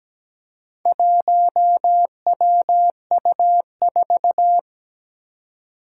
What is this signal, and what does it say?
Morse code, heard as a keyed tone, about 17 wpm, 700 Hz. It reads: 1WU4
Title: Four Challenging Letter-Number Combinations 17wpm